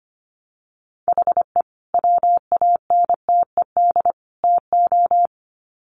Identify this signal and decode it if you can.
Morse code, heard as a keyed tone, about 25 wpm, 700 Hz. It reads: HE WANTED TO
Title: Sets of 3 Words 25wpm